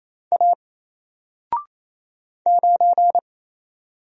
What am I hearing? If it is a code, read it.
Morse code, heard as a keyed tone, about 28 wpm, 700 Hz. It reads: A 9
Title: Single Letter-Number Alt 28wpm